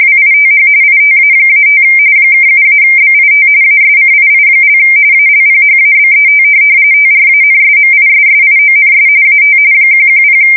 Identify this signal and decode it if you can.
RTTY, about 45.45 baud. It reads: WITH ITA-5 (7-BIT ASCII) ALPHABET TRANSMISSIONS AT 110
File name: fsk170 / teletype-RTTY